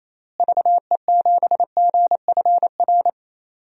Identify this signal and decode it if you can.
Morse code, heard as a keyed tone, about 27 wpm, 700 Hz. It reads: VE7GFR
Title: DXCC-100 Call Signs - 6 Characters - Repeated 3x - 28wpm